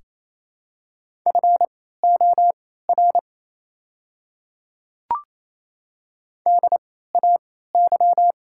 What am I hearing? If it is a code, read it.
Morse code, heard as a keyed tone, about 27 wpm, spaced as for 16 wpm, 690 Hz. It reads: FOR DAY
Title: 3x Character Spacing - Top 200 Words - 28wpm